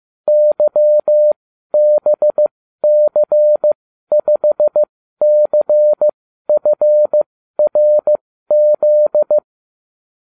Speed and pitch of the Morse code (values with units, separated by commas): 15 wpm, spaced as for 12 wpm, 600 Hz